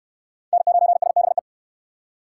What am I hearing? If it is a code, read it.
Morse code, heard as a keyed tone, about 65 wpm, 690 Hz. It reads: N9SBE